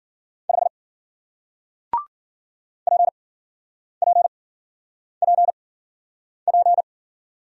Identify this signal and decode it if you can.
Morse code, heard as a keyed tone, about 54 wpm, spaced as for 9 wpm, 700 Hz. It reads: 5 PPPP